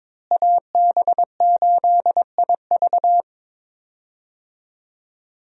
Morse code, heard as a keyed tone, about 22 wpm, 700 Hz. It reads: AB8IV
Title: USA Call Signs - 5 Characters - Repeated 3x - 22wpm